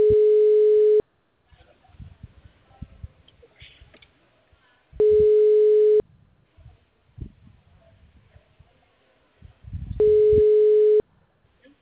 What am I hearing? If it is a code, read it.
Ringback tone (Europe, 425 Hz).